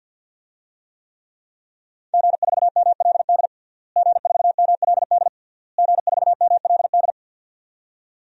Morse code, heard as a keyed tone, about 52 wpm, 700 Hz. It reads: G4MLD G4MLD G4MLD
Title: DXCC-100 Call Signs - 5 Characters - Repeated 3x - 50wpm